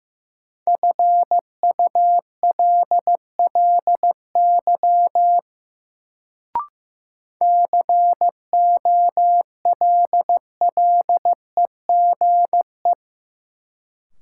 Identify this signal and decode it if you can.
Morse code, heard as a keyed tone, about 15 wpm, 700 Hz. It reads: FULLY COLLEGE
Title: Sentences from Top 1300 Words - New Word Drill - 15wpm